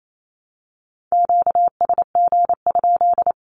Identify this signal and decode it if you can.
Morse code, heard as a keyed tone, about 28 wpm, 700 Hz. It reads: QSG?
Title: Q-Codes 28wpm